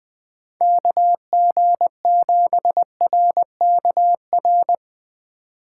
Morse code, heard as a keyed tone, about 21 wpm, 700 Hz. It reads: KG7RKR
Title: Speed Racing - USA 2x3 Call Signs 17wpm